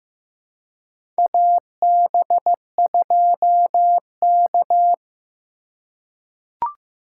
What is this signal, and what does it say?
Morse code, heard as a keyed tone, about 15 wpm, 700 Hz. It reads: AB2K
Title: USA Call Signs - 4 Characters - 2x Character Spacing - 15wpm